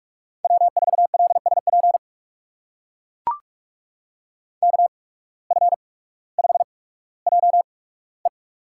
Morse code, heard as a keyed tone, about 43 wpm, spaced as for 19 wpm, 700 Hz. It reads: W4LSP KF5JE
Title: USA Call Signs - 5 Characters - 8x Character Spacing - 45wpm